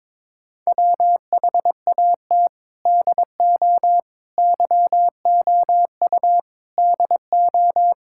Morse code, heard as a keyed tone, about 22 wpm, 700 Hz. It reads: WHAT DO YOU DO
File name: practice / 22wpm / Sentences with 4-Word Phrases - 22wpm